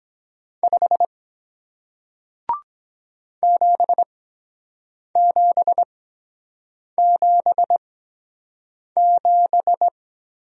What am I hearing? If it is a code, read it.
Morse code, heard as a keyed tone, about 23 wpm, spaced as for 7 wpm, 700 Hz. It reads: 5 7777